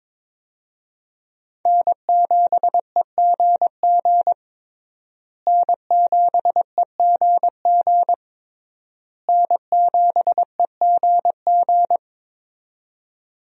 Morse code, heard as a keyed tone, about 22 wpm, 700 Hz. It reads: N7EGG N7EGG N7EGG
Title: USA Call Signs - 5 Characters - Repeated 3x - 22wpm